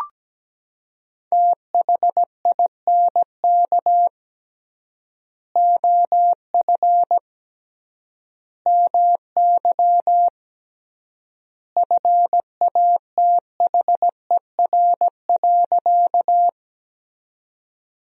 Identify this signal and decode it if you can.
Morse code, heard as a keyed tone, about 17 wpm, 700 Hz. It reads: THINK OF MY FATHER.
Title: 3x Word-Spacing - Sentences from Top 300 Words - 17wpm